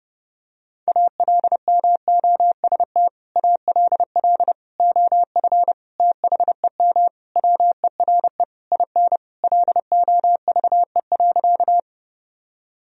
Morse code, heard as a keyed tone, about 30 wpm, 700 Hz. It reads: ALMOST ALL OF THEM WERE IN LOVE.
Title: Sentences from Top 1000 Words - Review Part 4 - 30wpm